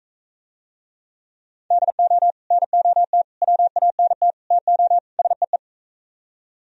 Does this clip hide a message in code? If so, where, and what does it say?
Yes: Morse code from 1.7 to 5.6 s, heard as a keyed tone, about 43 wpm, 700 Hz. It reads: DO NOT WANT TO SEE